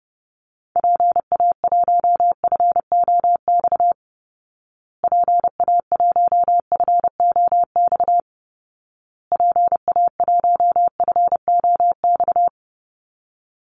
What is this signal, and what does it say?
Morse code, heard as a keyed tone, about 30 wpm, 700 Hz. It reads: PA1FOX PA1FOX PA1FOX
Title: Call Signs Repeated 2x - 30wpm